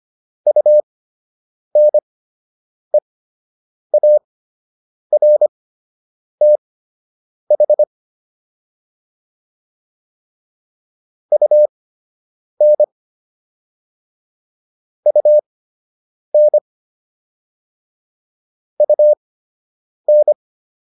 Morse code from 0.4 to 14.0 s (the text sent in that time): UNEARTH UN